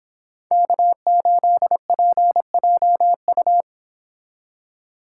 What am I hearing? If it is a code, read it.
Morse code, heard as a keyed tone, about 29 wpm, 700 Hz. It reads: K8PJU